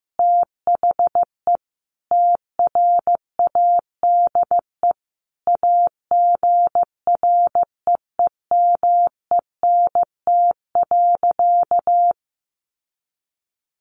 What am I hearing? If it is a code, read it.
Morse code, heard as a keyed tone, about 15 wpm, 700 Hz. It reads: THE TRADE AGREEMENT.
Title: Sentences from Top 1500 Words 15wpm